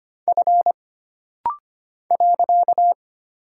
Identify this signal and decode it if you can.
Morse code, heard as a keyed tone, about 25 wpm, 700 Hz. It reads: F .
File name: Single Character (Letter:Number:Punctuation:Prosign) - Rapid-Fire - 25wpm